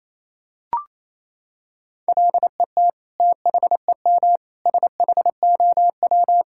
Morse code, heard as a keyed tone, about 28 wpm, 700 Hz. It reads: LET THEM SHOW
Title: Sentences from Top 1000 Words - Review Part 5 - 28wpm